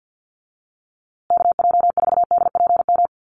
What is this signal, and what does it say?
Morse code, heard as a keyed tone, about 50 wpm, 700 Hz. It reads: KJ4DPM
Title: USA Call Signs Repeated 2x 50wpm